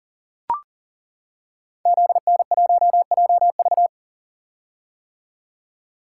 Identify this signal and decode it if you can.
Morse code, heard as a keyed tone, about 42 wpm, 700 Hz. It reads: ZN1JV
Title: Code Groups - 5 Letter:Numbers - 40wpm